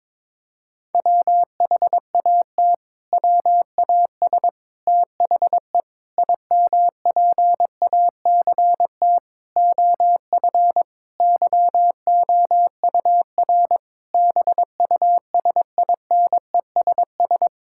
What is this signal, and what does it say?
Morse code, heard as a keyed tone, about 22 wpm, 700 Hz. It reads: WHAT WAS THE IMPACT OF YOUR BUSINESS
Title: Sentences from Top 1000 Words - Review Part 6 - 22wpm